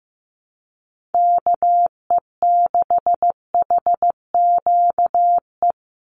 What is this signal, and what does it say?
Morse code, heard as a keyed tone, about 15 wpm, 700 Hz. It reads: KE6HQE
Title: USA Call Signs 15wpm